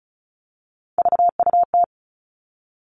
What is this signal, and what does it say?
Morse code, heard as a keyed tone, about 34 wpm, 700 Hz. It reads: VUT
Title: All Cut Numbers - 3-Digit 35wpm